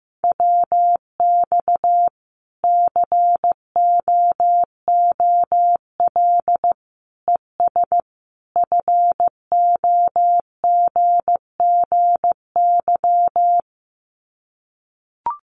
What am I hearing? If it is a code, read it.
Morse code, heard as a keyed tone, about 14 wpm, 700 Hz. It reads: WX COOL ES FOGGY
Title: Common QSO Phrases 15wpm